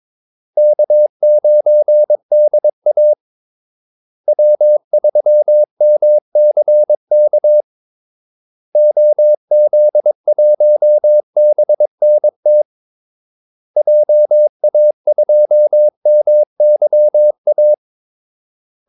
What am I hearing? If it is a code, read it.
Morse code, heard as a keyed tone, about 22 wpm, 600 Hz. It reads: K9DA W3MCK OZ1BNT JA2MYA
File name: Callsigns-22wpm